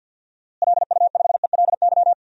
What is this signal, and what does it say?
Morse code, heard as a keyed tone, about 50 wpm, 700 Hz. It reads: LU5ELY
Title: DXCC-100 Call Signs - 6 Characters - Repeated 2x - 50wpm